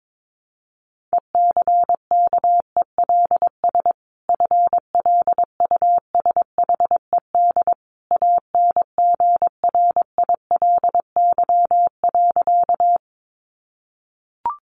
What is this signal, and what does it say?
Morse code, heard as a keyed tone, about 22 wpm, 700 Hz. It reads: ECKELS FLUSHED ANGRILY.
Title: A Sound Of Thunder 22wpm